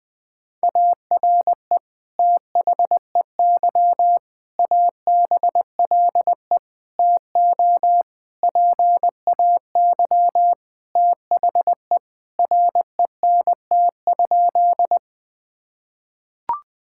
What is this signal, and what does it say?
Morse code, heard as a keyed tone, about 20 wpm, 700 Hz. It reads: ARE THEY ABLE TO PAY THE RENT?
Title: Sentences from Top 1000 Words - Review Part 5 - 20wpm